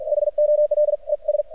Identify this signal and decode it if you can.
Morse code, heard as a keyed tone, about 52 wpm, spaced as for 42 wpm, 600 Hz. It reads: HOLES
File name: archived / CRSnet